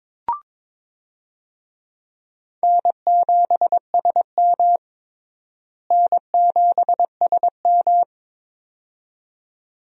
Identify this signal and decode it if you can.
Morse code, heard as a keyed tone, about 22 wpm, 700 Hz. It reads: N7SM N7SM
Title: USA Call Signs - 4 Characters - Repeated 2x - 22wpm